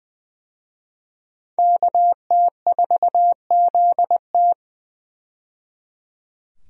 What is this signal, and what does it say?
Morse code, heard as a keyed tone, about 20 wpm, 700 Hz. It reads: KT4ZT
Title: USA Call Signs - 5 Characters - 20wpm